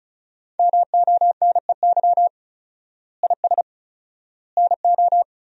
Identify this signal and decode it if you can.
Morse code, heard as a keyed tone, about 34 wpm, 700 Hz. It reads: MONEY IS NO
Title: 4x Word-Spacing - Sentences from Top 500 Words - 35wpm